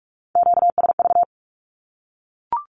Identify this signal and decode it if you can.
Morse code, heard as a keyed tone, about 44 wpm, 700 Hz. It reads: QSV
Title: Q-Codes 45wpm